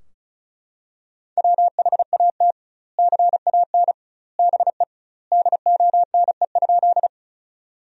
Morse code, heard as a keyed tone, about 36 wpm, 700 Hz. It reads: WHAT CAN BE DONE?
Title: 2x Word-Spacing - Sentences from Top 500 Words - 35wpm